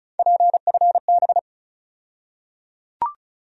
Morse code, heard as a keyed tone, about 36 wpm, 700 Hz. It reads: PFB
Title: USA Call Sign Postfixes - 3 Letters 35wpm